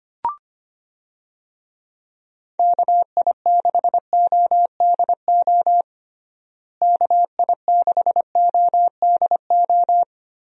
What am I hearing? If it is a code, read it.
Morse code, heard as a keyed tone, about 25 wpm, 700 Hz. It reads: KI6ODO KI6ODO
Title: USA Call Signs - 6 Characters - Repeated 3x - 25wpm